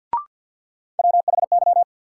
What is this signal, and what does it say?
Morse code, heard as a keyed tone, about 53 wpm, 700 Hz. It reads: WHY
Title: Sets of 3 Words - New Word Drill - 50wpm